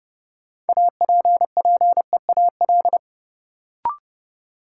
Morse code, heard as a keyed tone, about 30 wpm, 700 Hz. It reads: APPEAL
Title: Sentences from Top 1300 Words - New Word Drill - 30wpm